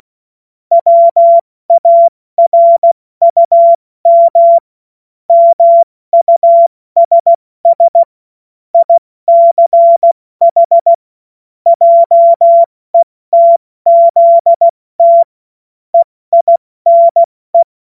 Morse code, heard as a keyed tone, about 16 wpm, 680 Hz. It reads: WARUM MUSS ICH JETZT EINE
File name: Witz-16-September-2025-70-F80